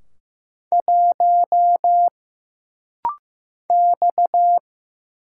Morse code, heard as a keyed tone, about 15 wpm, 700 Hz. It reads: 1 X
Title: Single Letter-Number Alt - Rapid-Fire 15wpm